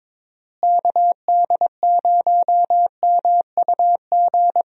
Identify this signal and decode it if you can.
Morse code, heard as a keyed tone, about 22 wpm, 700 Hz. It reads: KD0MUG
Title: USA Call Signs - 6 Characters - Repeated 3x - 22wpm